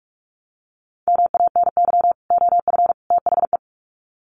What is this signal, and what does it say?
Morse code, heard as a keyed tone, about 44 wpm, 700 Hz. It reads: MANY OF THE
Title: Sets of 3 Words 45wpm